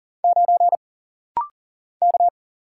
Morse code, heard as a keyed tone, about 40 wpm, 700 Hz. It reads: 9 K
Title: Single Letter-Number - Rapid-Fire 40wpm